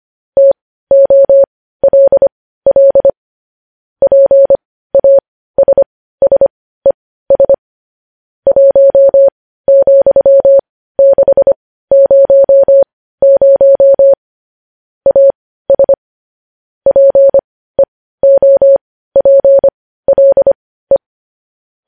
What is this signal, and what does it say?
Morse code, heard as a keyed tone, about 25 wpm, spaced as for 15 wpm, 550 Hz. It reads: TOLL PASSES 1,600 AS PEOPLE
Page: News Headlines in Morse Code at 15 WPM